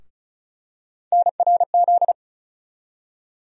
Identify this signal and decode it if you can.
Morse code, heard as a keyed tone, about 34 wpm, 700 Hz. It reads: NRZ
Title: USA Call Sign Postfixes - 3 Letters 35wpm